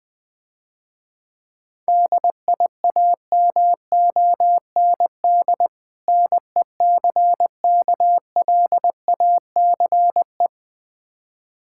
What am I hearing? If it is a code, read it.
Morse code, heard as a keyed tone, about 20 wpm, 700 Hz. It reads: DIAMOND NECKLACE
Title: Difficult Words - Speed-Racing with Hints - 20wpm